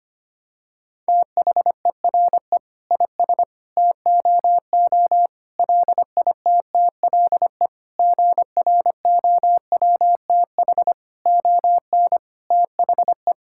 Morse code, heard as a keyed tone, about 25 wpm, 700 Hz. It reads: THERE IS TOO LITTLE GROWTH ON THE
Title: Sentences from Top 1000 Words - Review Part 5 - 25wpm